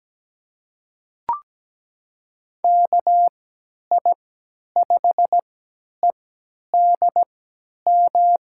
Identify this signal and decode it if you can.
Morse code, heard as a keyed tone, about 17 wpm, spaced as for 10 wpm, 700 Hz. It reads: KI5EDM
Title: USA Call Signs - 6 Characters - 3x Character Spacing - 17wpm